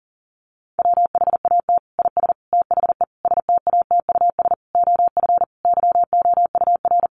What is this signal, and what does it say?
Morse code, heard as a keyed tone, about 40 wpm, 700 Hz. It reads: WHAT IS THE STATUS OF YOUR
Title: Sentences from Top 1400 Words 40wpm